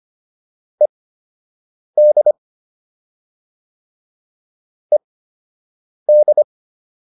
Morse code, heard as a keyed tone, about 25 wpm, spaced as for 7 wpm, 600 Hz. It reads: ED ED